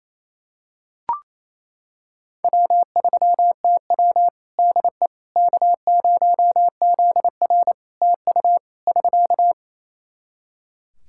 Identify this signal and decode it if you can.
Morse code, heard as a keyed tone, about 27 wpm, 700 Hz. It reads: W3TW DE K0ZR TU <SK>
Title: K1USN SST QSO Practice 28wpm